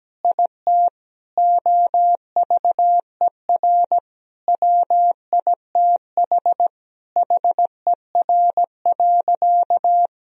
Morse code, heard as a keyed tone, about 17 wpm, 700 Hz. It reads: IT OVER WITH HER.
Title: Sentences from Top 1000 Words - Review Part 2 - 17wpm